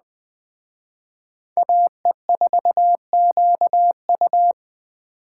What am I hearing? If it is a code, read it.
Morse code, heard as a keyed tone, about 23 wpm, 700 Hz. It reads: AE4QU
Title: Speed Racing - USA Call Signs 20wpm